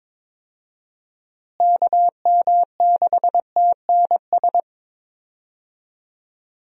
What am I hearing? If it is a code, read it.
Morse code, heard as a keyed tone, about 22 wpm, 700 Hz. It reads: KM6TNS
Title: USA Call Signs - 6 Characters - Repeated 2x - 22wpm